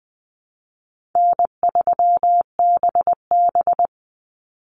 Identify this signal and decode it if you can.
Morse code, heard as a keyed tone, about 20 wpm, 700 Hz. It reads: N3BB
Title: Easy Call Signs Repeated 3x - 20wpm